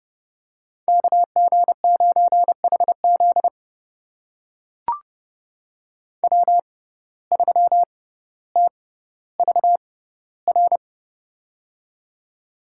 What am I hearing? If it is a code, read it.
Morse code, heard as a keyed tone, about 30 wpm, spaced as for 15 wpm, 700 Hz. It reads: KG9HZ W3TVR
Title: USA Call Signs - 5 Characters - 6x Character Spacing - 30wpm